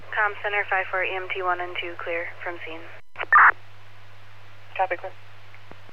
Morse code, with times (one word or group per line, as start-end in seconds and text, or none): none